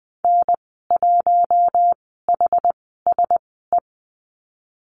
Morse code, heard as a keyed tone, about 20 wpm, spaced as for 14 wpm, 700 Hz. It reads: N1HSE
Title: 082 Lesson 14g - Number 2 - Call Signs 20wpm